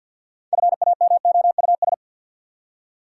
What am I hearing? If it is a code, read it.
Morse code, heard as a keyed tone, about 49 wpm, 700 Hz. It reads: FAMOUS
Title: Six Letter Words - 50wpm